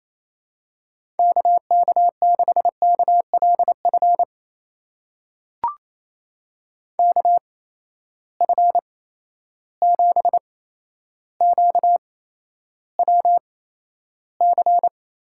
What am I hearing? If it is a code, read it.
Morse code, heard as a keyed tone, about 28 wpm, spaced as for 12 wpm, 700 Hz. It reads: KK6KLF KF7QWC
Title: USA Call Signs - 6 Characters - 8x Character Spacing - 28wpm